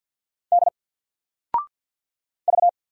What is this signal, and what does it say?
Morse code, heard as a keyed tone, about 50 wpm, 700 Hz. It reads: D V
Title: Single Character (Letter:Number:Punctuation:Prosign) 50wpm